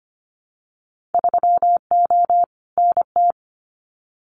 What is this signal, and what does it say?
Morse code, heard as a keyed tone, about 25 wpm, 700 Hz. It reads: 3O NT
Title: Winter Field Day Exchanges Encore 25wpm